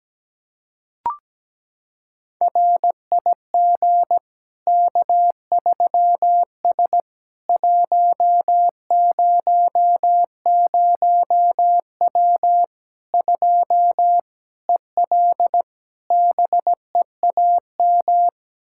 Morse code, heard as a keyed tone, about 17 wpm, 700 Hz. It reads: RIG K3S 100W 2 EL BEAM
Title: Practice QSO - Exchange 2 - Info - 17wpm